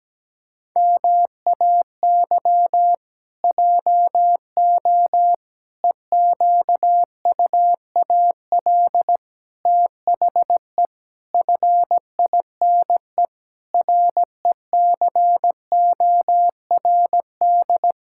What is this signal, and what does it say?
Morse code, heard as a keyed tone, about 17 wpm, 700 Hz. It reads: MAY JO EQUAL THE FINE RECORD
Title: Pangram Challenge v2 - 17wpm